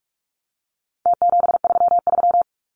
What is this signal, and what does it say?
Morse code, heard as a keyed tone, about 44 wpm, 700 Hz. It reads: T733
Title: Numbers - Contest Style - 4-Digit 45wpm